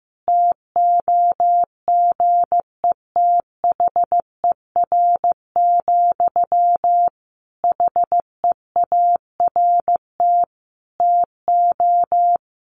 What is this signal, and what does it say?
Morse code, heard as a keyed tone, about 14 wpm, 700 Hz. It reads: TOGETHER, HEART TO
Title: Sentences from Top 1500 Words 15wpm